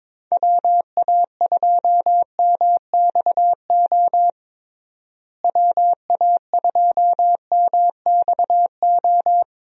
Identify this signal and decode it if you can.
Morse code, heard as a keyed tone, about 22 wpm, 700 Hz. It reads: WA2MXO WA2MXO
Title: USA Call Signs - 6 Characters - Repeated 2x - 22wpm